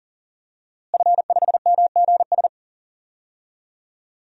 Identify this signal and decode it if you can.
Morse code, heard as a keyed tone, about 38 wpm, 700 Hz. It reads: F5MGS
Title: DXCC-100 Call Signs - 5 Characters - Repeated 3x - 40wpm